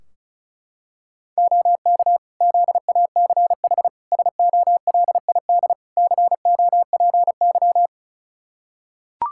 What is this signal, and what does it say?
Morse code, heard as a keyed tone, about 34 wpm, 700 Hz. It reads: OK ZACH SOLID COPY
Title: K1USN SST QSO Practice 35wpm